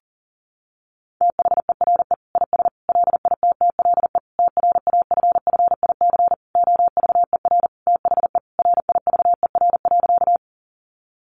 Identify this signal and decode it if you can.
Morse code, heard as a keyed tone, about 40 wpm, 700 Hz. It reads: THERE IS LITTLE TRAFFIC OVER THE RIVER.
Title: Sentences from Top 1300 Words 40wpm